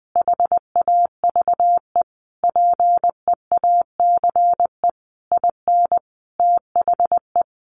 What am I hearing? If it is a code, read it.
Morse code, heard as a keyed tone, about 20 wpm, 700 Hz. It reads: HAVE PEACE IN THE
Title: Sentences from Top 1500 Words 20wpm